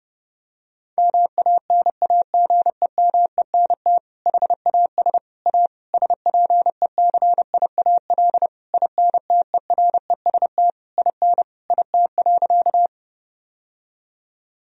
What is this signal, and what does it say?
Morse code, heard as a keyed tone, about 30 wpm, 700 Hz. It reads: MANAGEMENT HAS A SPECIAL INTEREST IN IT.
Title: Sentences from Top 1000 Words - Review Part 4 - 30wpm